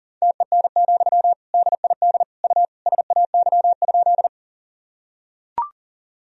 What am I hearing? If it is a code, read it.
Morse code, heard as a keyed tone, about 40 wpm, 700 Hz. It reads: TEN, DID U SAY?
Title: Sentences from Top 400 Words and Abbreviations 40wpm